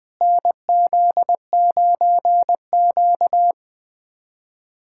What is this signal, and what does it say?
Morse code, heard as a keyed tone, about 20 wpm, 700 Hz. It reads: NZ9Q
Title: USA Callsign DE Callsign - 4 Characters - 20wpm